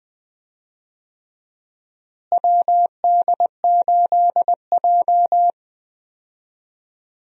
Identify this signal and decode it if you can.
Morse code, heard as a keyed tone, about 20 wpm, 700 Hz. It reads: WD8J
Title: USA Call Signs - 4 Characters - Repeated 3x - 20wpm